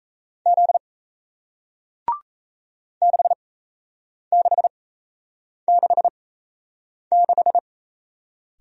Morse code, heard as a keyed tone, about 37 wpm, spaced as for 8 wpm, 700 Hz. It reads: Z 6666